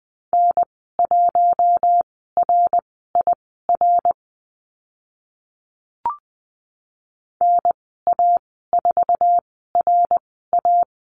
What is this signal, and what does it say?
Morse code, heard as a keyed tone, about 20 wpm, spaced as for 14 wpm, 700 Hz. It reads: N1RIR NA4RA
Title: 062 Lesson 11g - Letter h - Call Signs 20wpm